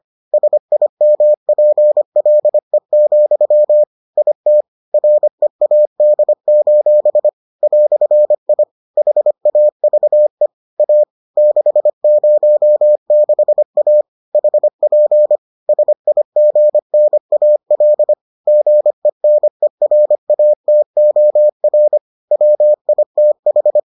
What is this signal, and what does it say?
Morse code, heard as a keyed tone, about 25 wpm, 600 Hz. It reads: SIMPLE, IT READ: "I HAVE A 606A HP SIGNAL GENERATOR WITH